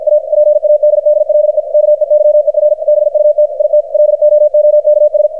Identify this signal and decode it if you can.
Morse code, heard as a keyed tone, about 52 wpm, 600 Hz, with radio noise. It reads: A PADDLE KEYER BUT IT LOOKS
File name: archived / CRSnet